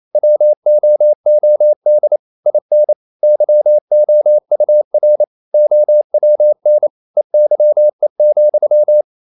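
Morse code, heard as a keyed tone, about 28 wpm, 600 Hz. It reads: WOOD IN YOUR OWN EYE,
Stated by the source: Morse code practice at 28 wpm, text of Ken Follett's novel Never in English, from CW med Gnister.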